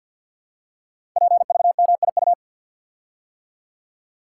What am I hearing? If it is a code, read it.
Morse code, heard as a keyed tone, about 50 wpm, 700 Hz. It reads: PVMIU
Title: Code Groups - 5 Letter:Numbers - 50wpm